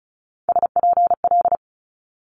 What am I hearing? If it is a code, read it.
Morse code, heard as a keyed tone, about 36 wpm, 700 Hz. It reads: SPL
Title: Multi-letter Phonemes Quick Fire 35wpm